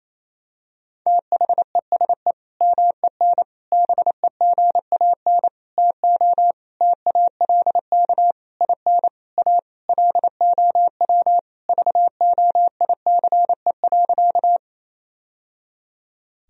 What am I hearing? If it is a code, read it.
Morse code, heard as a keyed tone, about 28 wpm, 700 Hz. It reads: THESE MEN BEGAN TO TALK IN A LOW VOICE.
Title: Sentences from Top 1000 Words - Review Part 1 - 28wpm